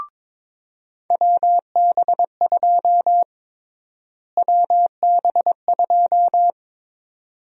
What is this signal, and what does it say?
Morse code, heard as a keyed tone, about 22 wpm, 700 Hz. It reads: WB2 WB2
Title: USA Call Sign Prefixes - Repeated 2x - 22wpm